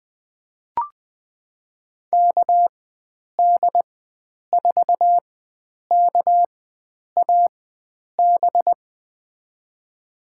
Morse code, heard as a keyed tone, about 20 wpm, spaced as for 9 wpm, 700 Hz. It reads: KD4KAB
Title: USA Call Signs - 4x Character Spacing - 20wpm